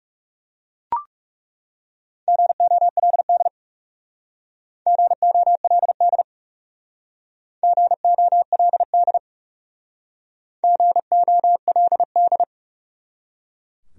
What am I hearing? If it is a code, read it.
Morse code, heard as a keyed tone, about 37 wpm, 700 Hz. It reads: GOLD GOLD GOLD GOLD
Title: Speed-Racing Top 500 Words 30wpm